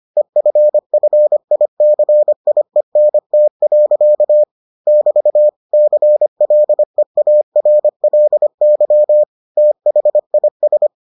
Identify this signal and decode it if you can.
Morse code, heard as a keyed tone, about 25 wpm, 600 Hz. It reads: EFFICIENT. <BT> CLEARLY THIS